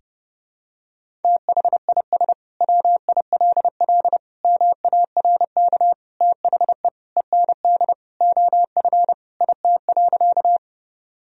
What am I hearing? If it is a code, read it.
Morse code, heard as a keyed tone, about 30 wpm, 700 Hz. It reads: THIS WILL MARK THE END OF IT.
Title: Sentences from Top 1000 Words - Review Part 1 - 30wpm